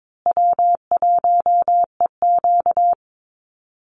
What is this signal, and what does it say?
Morse code, heard as a keyed tone, about 22 wpm, 700 Hz. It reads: W1EQ
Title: Easy Call Signs Repeated 2x - 22wpm